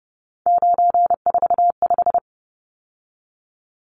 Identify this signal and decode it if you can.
Morse code, heard as a keyed tone, about 30 wpm, 700 Hz. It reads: 945
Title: Numbers - 3-Digit 30wpm